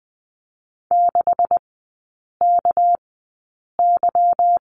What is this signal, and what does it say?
Morse code, heard as a keyed tone, about 20 wpm, spaced as for 8 wpm, 690 Hz. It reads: 6KY
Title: 205 Lesson 33e - Letter j - Three Characters 20wpm